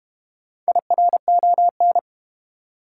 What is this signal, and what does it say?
Morse code, heard as a keyed tone, about 31 wpm, 700 Hz. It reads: IRON
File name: Easy Words - Speed-Racing with Hints - 25wpm